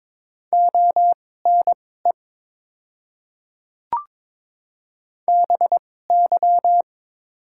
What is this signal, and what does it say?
Morse code, heard as a keyed tone, about 22 wpm, spaced as for 16 wpm, 700 Hz. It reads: ONE BY
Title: 2x Character Spacing - Top 100 Words - 22wpm